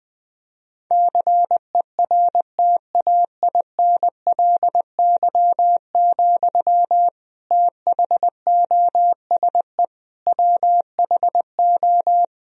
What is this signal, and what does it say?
Morse code, heard as a keyed tone, about 20 wpm, 700 Hz. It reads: CERTAINLY, THOSE WHO
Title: Sentences from Top 1000 Words - Review Part 2 - 20wpm